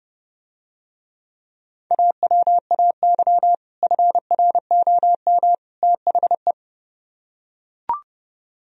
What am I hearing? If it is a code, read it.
Morse code, heard as a keyed tone, about 29 wpm, 700 Hz. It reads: AWAY FROM THE